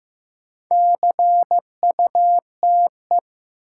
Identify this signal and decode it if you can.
Morse code, heard as a keyed tone, about 15 wpm, 700 Hz. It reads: CUTE
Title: Sentences from Top 1000 Words - Review Part 7 - New Word Drill - 15wpm